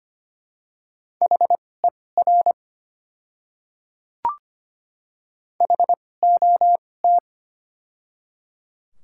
Morse code, heard as a keyed tone, about 25 wpm, spaced as for 18 wpm, 690 Hz. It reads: HER HOT
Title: 2x Character Spacing - Top 100 Words - 25wpm